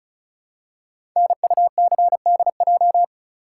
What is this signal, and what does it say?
Morse code, heard as a keyed tone, about 36 wpm, 700 Hz. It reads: NUCDJ
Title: Code Groups - 5 Letters - 35wpm